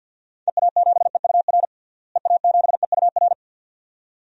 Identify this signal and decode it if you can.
Morse code, heard as a keyed tone, about 49 wpm, 700 Hz. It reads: EA7EUR EA7EUR
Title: DXCC-100 Call Signs - 6 Characters - Repeated 2x - 50wpm